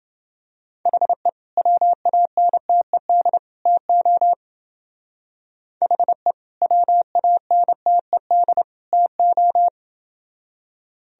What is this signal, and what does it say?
Morse code, heard as a keyed tone, about 29 wpm, 700 Hz. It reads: HE WANTED TO HE WANTED TO